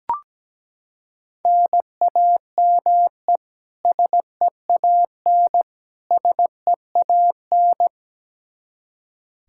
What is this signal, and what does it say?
Morse code, heard as a keyed tone, about 17 wpm, 700 Hz. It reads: NAME SEAN SEAN
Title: Practice QSO - Exchange 1 - Beginning - 17wpm